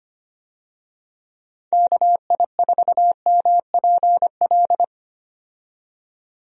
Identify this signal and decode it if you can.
Morse code, heard as a keyed tone, about 25 wpm, 700 Hz. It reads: KI4MPL
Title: USA Call Signs - 6 Characters - 25wpm